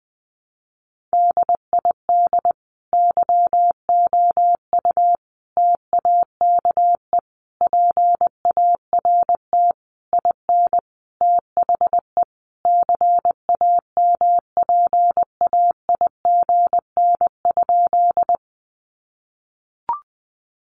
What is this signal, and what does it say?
Morse code, heard as a keyed tone, about 20 wpm, 700 Hz. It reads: DID YOU TAKE PART IN THE CAMPAIGN?
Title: Sentences from Top 1200 Words 20wpm